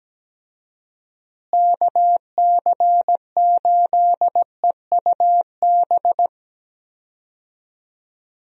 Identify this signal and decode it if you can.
Morse code, heard as a keyed tone, about 17 wpm, 700 Hz. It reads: KC8EUB
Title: USA Call Signs - 6 Characters - Repeated 2x - 17wpm